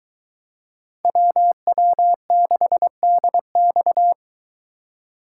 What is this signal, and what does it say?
Morse code, heard as a keyed tone, about 21 wpm, 700 Hz. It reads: WW6DX